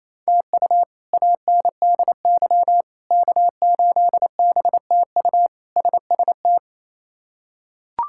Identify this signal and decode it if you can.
Morse code, heard as a keyed tone, about 28 wpm, 700 Hz. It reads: TU ANDY K8BTU SST
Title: K1USN SST QSO Practice - 28wpm